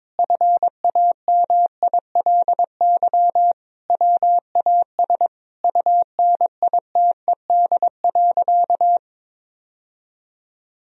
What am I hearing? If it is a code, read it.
Morse code, heard as a keyed tone, about 22 wpm, 700 Hz. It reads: FAMILY WAS UNITED.
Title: Sentences from Top 1000 Words - Review Part 4 - 22wpm